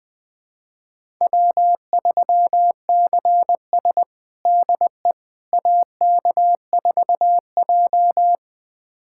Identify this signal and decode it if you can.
Morse code, heard as a keyed tone, about 21 wpm, 700 Hz. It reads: W3CS DE AK4J
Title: USA Callsign DE Callsign - 4 Characters - 20wpm